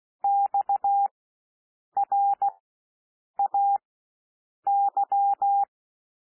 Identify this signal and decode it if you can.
Morse code, heard as a keyed tone, about 16 wpm, spaced as for 7 wpm, 800 Hz. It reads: XRAY
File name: cwmp3 / W1 SOUND WORDS 16 wpm